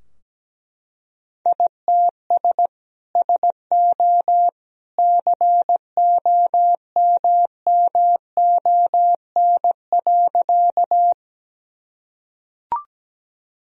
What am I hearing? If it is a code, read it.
Morse code, heard as a keyed tone, about 17 wpm, 700 Hz. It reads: ITS SO COMMON.
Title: Sentences from Top 1000 Words - Review Part 3 - 17wpm